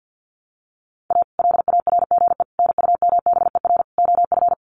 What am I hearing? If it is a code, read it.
Morse code, heard as a keyed tone, about 50 wpm, 700 Hz. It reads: A LARGE NUMBER OF
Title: Sets of 4 Words 50wpm